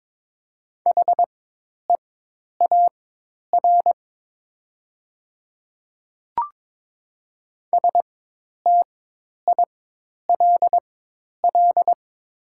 Morse code, heard as a keyed tone, about 22 wpm, spaced as for 10 wpm, 700 Hz. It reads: HEAR STILL
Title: 4x Character Spacing - Top 300 Words - 22wpm